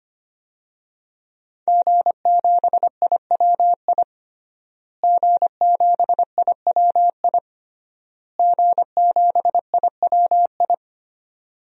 Morse code, heard as a keyed tone, about 25 wpm, 700 Hz. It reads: G7IWI G7IWI G7IWI
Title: DXCC-100 Call Signs - 5 Characters - Repeated 3x - 25wpm